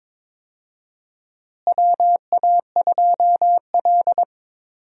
Morse code, heard as a keyed tone, about 22 wpm, 700 Hz. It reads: WA2L
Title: USA Call Signs - 4 Characters - Repeated 2x - 22wpm